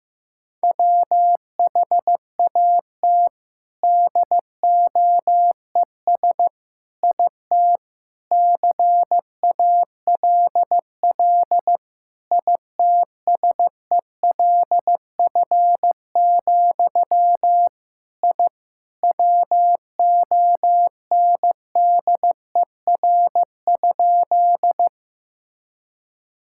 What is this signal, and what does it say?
Morse code, heard as a keyed tone, about 15 wpm, 700 Hz. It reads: WHAT DOES IT CALL ITSELF, I WONDER?
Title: Sentences from Top 1000 Words - Review Part 5 - 15wpm